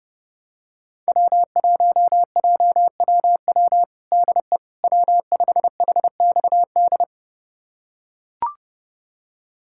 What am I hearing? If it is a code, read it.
Morse code, heard as a keyed tone, about 30 wpm, 700 Hz. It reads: W1JWW DE W5HXD
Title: USA Callsign DE Callsign 30wpm